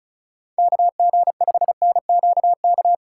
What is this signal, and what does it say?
Morse code, heard as a keyed tone, about 34 wpm, 700 Hz. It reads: KG5NQK
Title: USA Call Signs - 6 Characters - 4x Character Spacing - 35wpm